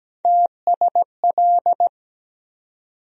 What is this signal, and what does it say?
Morse code, heard as a keyed tone, about 17 wpm, 700 Hz. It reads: TSL
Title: USA Call Sign Postfixes - 3 Letters 17wpm